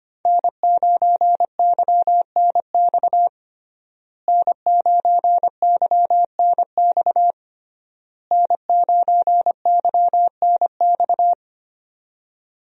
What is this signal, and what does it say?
Morse code, heard as a keyed tone, about 25 wpm, 700 Hz. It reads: N9YNX N9YNX N9YNX
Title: USA Call Signs - 5 Characters - Repeated 3x - 25wpm